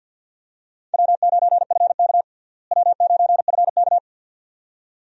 Morse code, heard as a keyed tone, about 52 wpm, 700 Hz. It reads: W9FK W9FK
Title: USA Call Signs - 4 Characters - Repeated 2x - 50wpm